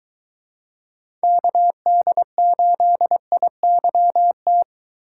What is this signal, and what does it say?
Morse code, heard as a keyed tone, about 21 wpm, 700 Hz. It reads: KD8IYT